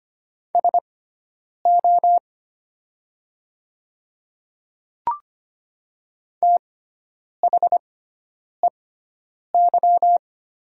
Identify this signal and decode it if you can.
Morse code, heard as a keyed tone, about 25 wpm, spaced as for 9 wpm, 700 Hz. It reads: SO THEY
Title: 6x Character Spacing - Top 300 Words - 25wpm